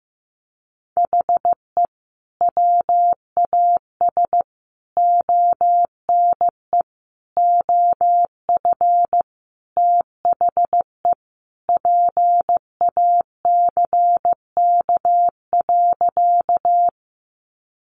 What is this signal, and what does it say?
Morse code, heard as a keyed tone, about 15 wpm, 700 Hz. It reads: HE WAS ONE OF THE PACK.
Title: Sentences from Top 1300 Words 15wpm